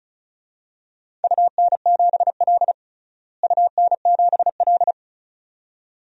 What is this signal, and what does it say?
Morse code, heard as a keyed tone, about 36 wpm, 690 Hz. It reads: UN7L UN7L
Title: DXCC-100 Call Signs - 4 Characters - Repeated 2x - 35wpm